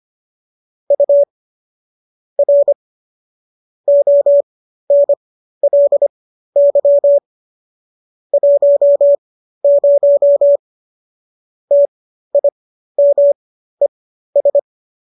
Morse code, heard as a keyed tone, about 25 wpm, spaced as for 13 wpm, 570 Hz. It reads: U R ONLY 10 TIMES